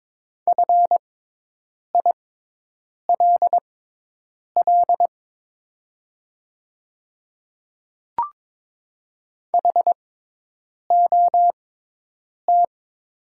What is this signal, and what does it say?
Morse code, heard as a keyed tone, about 22 wpm, spaced as for 8 wpm, 700 Hz. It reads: FILL HOT
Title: 6x Character Spacing - Top 500 Words - 22wpm